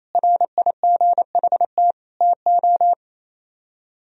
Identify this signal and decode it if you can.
Morse code, heard as a keyed tone, about 35 wpm, 700 Hz. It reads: RIGHT TO
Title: Speed-Racing - Sets of 4 Words 28wpm